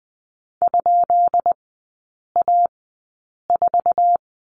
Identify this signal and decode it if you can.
Morse code, heard as a keyed tone, about 19 wpm, spaced as for 8 wpm, 700 Hz. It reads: ?A4
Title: 133 Lesson 22e - Question Mark - Three Characters 20wpm